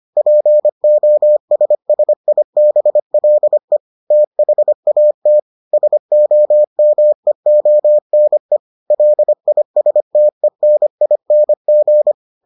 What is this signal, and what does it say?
Morse code, heard as a keyed tone, about 25 wpm, 600 Hz. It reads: POSSIBLE THAT SOMEONE LISTENING